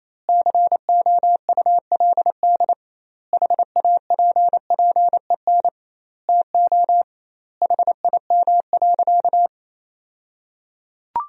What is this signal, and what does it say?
Morse code, heard as a keyed tone, about 28 wpm, 700 Hz. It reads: COULD HAPPEN TO HIM.
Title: 2x Word-Spacing - Sentences from Top 400 Words - 28wpm